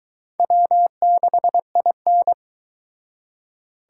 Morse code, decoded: W6IN

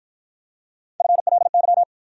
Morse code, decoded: FLY

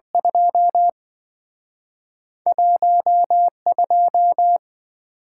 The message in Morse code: 21 2

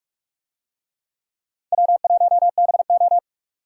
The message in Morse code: W1BO

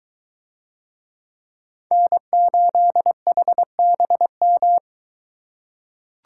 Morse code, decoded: N8HBM

